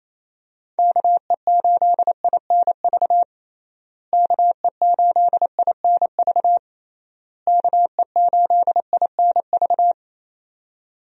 Morse code, heard as a keyed tone, about 28 wpm, 700 Hz. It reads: KE8INV KE8INV KE8INV
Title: USA Call Signs - 6 Characters - Repeated 3x - 28wpm